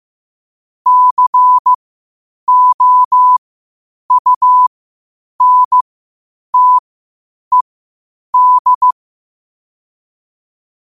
Morse code, heard as a keyed tone, about 15 wpm, spaced as for 8 wpm, 1000 Hz. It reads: COUNTED
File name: qotdfiles / 8wpm